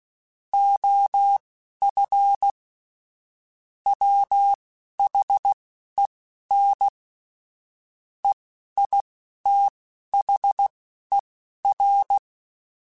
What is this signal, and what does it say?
Morse code, heard as a keyed tone, about 16 wpm, spaced as for 11 wpm, 800 Hz. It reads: OF WHEN EITHER
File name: Technical Subjects / Technical Subjects-mp3s / AND Gate @10